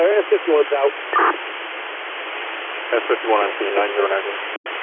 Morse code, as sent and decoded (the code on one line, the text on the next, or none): none